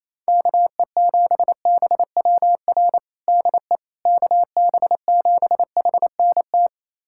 Morse code, heard as a keyed tone, about 28 wpm, 700 Hz. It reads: KE7BWR DE KB7HNT
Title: USA Callsign DE Callsign - 6 Characters - 28wpm